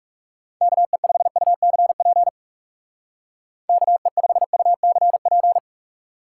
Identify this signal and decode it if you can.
Morse code, heard as a keyed tone, about 38 wpm, 700 Hz. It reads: KE5UCP KE5UCP